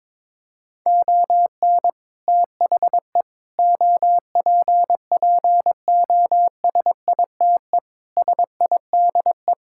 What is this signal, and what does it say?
Morse code, heard as a keyed tone, about 22 wpm, 700 Hz. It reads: ON THE OPPOSITE SIDE
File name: practice / 22wpm / Sets of 6 Words - 22wpm